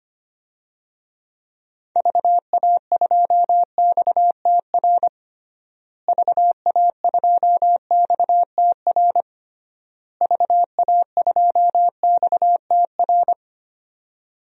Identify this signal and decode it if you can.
Morse code, heard as a keyed tone, about 25 wpm, 700 Hz. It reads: VA2XTR VA2XTR VA2XTR
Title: DXCC-100 Call Signs - 6 Characters - Repeated 3x - 25wpm